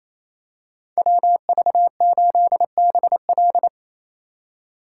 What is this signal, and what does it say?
Morse code, heard as a keyed tone, about 28 wpm, 700 Hz. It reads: WV8BL
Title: USA Call Signs - 5 Characters - 2x Character Spacing - 28wpm